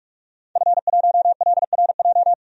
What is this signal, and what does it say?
Morse code, heard as a keyed tone, about 47 wpm, 700 Hz. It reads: F1LRJ
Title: DXCC-100 Call Signs - 5 Characters - 45wpm